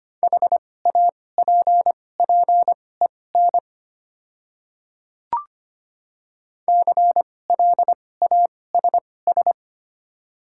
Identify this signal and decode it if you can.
Morse code, heard as a keyed tone, about 25 wpm, spaced as for 18 wpm, 700 Hz. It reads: HAPPEN CLASS
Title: 2x Character Spacing - Top 500 Words - 25wpm